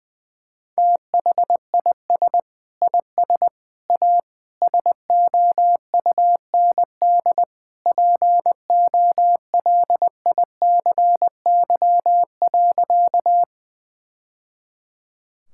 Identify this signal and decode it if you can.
Morse code, heard as a keyed tone, about 20 wpm, 700 Hz. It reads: THIS IS A SOUND POLICY.
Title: Sentences from Top 1000 Words - Review Part 5 - 20wpm